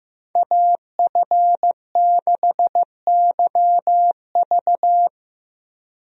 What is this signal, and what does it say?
Morse code, heard as a keyed tone, about 15 wpm, 700 Hz. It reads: AF6YV
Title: USA Call Signs - 5 Characters - Repeated 2x - 15wpm